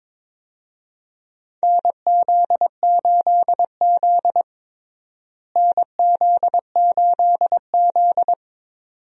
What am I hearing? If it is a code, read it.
Morse code, heard as a keyed tone, about 22 wpm, 700 Hz. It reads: NZ8Z NZ8Z
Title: USA Call Signs - 4 Characters - Repeated 2x - 22wpm